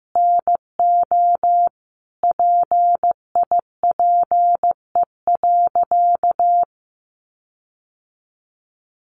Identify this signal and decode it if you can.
Morse code, heard as a keyed tone, about 15 wpm, 700 Hz. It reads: NO PIPE.
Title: Sentences from Top 1500 Words 15wpm